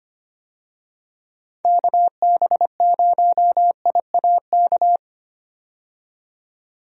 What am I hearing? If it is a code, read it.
Morse code, heard as a keyed tone, about 25 wpm, 700 Hz. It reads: KB0IAK
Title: USA Call Signs - 6 Characters - 25wpm